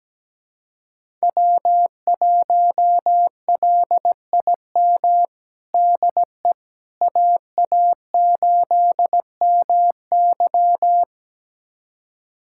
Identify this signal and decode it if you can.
Morse code, heard as a keyed tone, about 17 wpm, 700 Hz. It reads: W1LIM DE AA8MY
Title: USA Callsign DE Callsign - 5 Characters - 17wpm